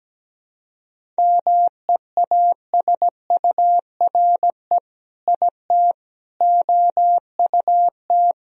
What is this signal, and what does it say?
Morse code, heard as a keyed tone, about 17 wpm, 700 Hz. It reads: MEASURE IT OUT
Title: Sentences from Top 1000 Words - Review Part 2 - 17wpm